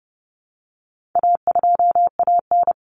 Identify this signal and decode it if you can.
Morse code, heard as a keyed tone, about 30 wpm, 700 Hz. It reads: A2AN
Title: Common Cut Numbers - 4-Digit 30wpm